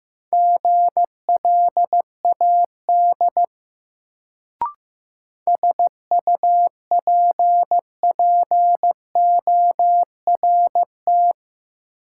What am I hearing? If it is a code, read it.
Morse code, heard as a keyed tone, about 15 wpm, 700 Hz. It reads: GLAD SUPPORT
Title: Sentences from Top 800 Words - New Word Drill - 15wpm